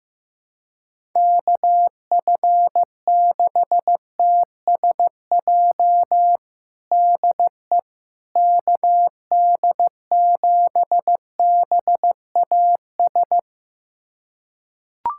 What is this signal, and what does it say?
Morse code, heard as a keyed tone, about 16 wpm, 700 Hz. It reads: KF6TSJ DE KD7BAS
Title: USA Callsign DE Callsign - 6 Characters - 15wpm